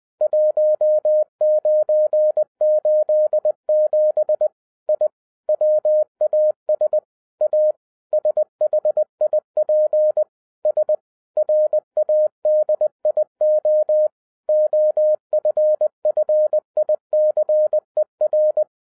Morse code, heard as a keyed tone, about 20 wpm, 610 Hz. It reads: 1987 I WAS A SHIP S RADIO OFFICER